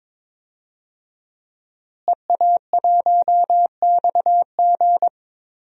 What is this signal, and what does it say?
Morse code, heard as a keyed tone, about 21 wpm, 700 Hz. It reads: EA1XG
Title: DXCC-100 Call Signs - 5 Characters - Repeated 2x - 22wpm